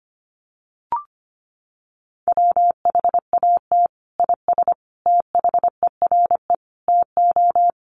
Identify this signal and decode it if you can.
Morse code, heard as a keyed tone, about 25 wpm, 700 Hz. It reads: WHAT IS THERE TO
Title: Sentences from Top 100 Words 25wpm